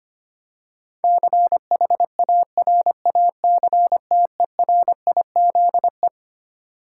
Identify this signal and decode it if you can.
Morse code, heard as a keyed tone, about 25 wpm, 700 Hz. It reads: CHARACTERIZE
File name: Twelve Letter Words - 25wpm